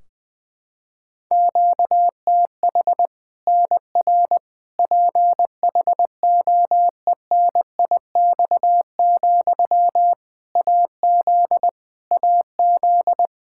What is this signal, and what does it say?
Morse code, heard as a keyed tone, about 21 wpm, 700 Hz. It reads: QTH NR PHOENIX, AZ AZ
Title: K1USN SST QSO Practice 20wpm